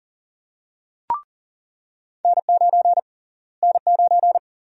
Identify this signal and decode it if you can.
Morse code, heard as a keyed tone, about 40 wpm, 700 Hz. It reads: N9 N9
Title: USA Call Sign Prefixes - Repeated 2x - 40wpm